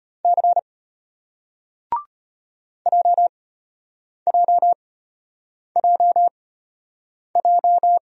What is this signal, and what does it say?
Morse code, heard as a keyed tone, about 33 wpm, spaced as for 8 wpm, 700 Hz. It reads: C JJJJ